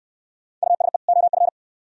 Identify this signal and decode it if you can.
Morse code, heard as a keyed tone, about 64 wpm, 700 Hz. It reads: USE OF